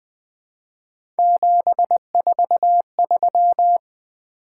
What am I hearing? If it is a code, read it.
Morse code, heard as a keyed tone, about 20 wpm, 700 Hz. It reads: 743
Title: Three Letter-Number Combinations - 20wpm